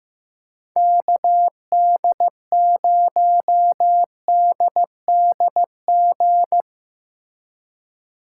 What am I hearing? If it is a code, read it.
Morse code, heard as a keyed tone, about 17 wpm, 700 Hz. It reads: KD0DDG
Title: Speed Racing - USA 2x3 Call Signs 15wpm